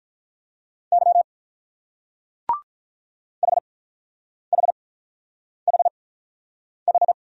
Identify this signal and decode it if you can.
Morse code, heard as a keyed tone, about 43 wpm, spaced as for 9 wpm, 690 Hz. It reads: Y HHHH
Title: Speed-Racing Single Letters 35wpm